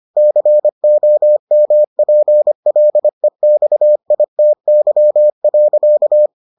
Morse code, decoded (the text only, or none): COMPLEXITY.